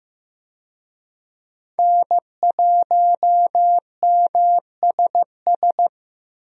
Morse code, heard as a keyed tone, about 16 wpm, 700 Hz. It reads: N1MSS
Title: USA Call Signs - 5 Characters - Repeated 2x - 15wpm